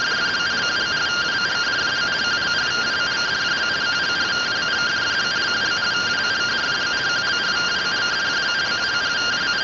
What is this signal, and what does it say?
Two-tone teleprinter signal, about 170 Hz shift.